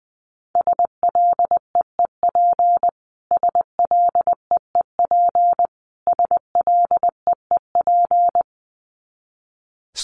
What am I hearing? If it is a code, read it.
Morse code, heard as a keyed tone, about 20 wpm, 700 Hz. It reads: SLEEP SLEEP SLEEP
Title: Top 500 Words Repeated 20wpm